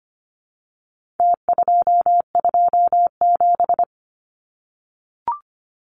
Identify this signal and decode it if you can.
Morse code, heard as a keyed tone, about 25 wpm, 700 Hz. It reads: T227
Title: Numbers - Contest Style - 4-Digit 25wpm